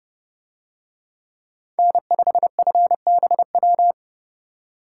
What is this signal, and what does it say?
Morse code, heard as a keyed tone, about 29 wpm, 700 Hz. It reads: N5FBW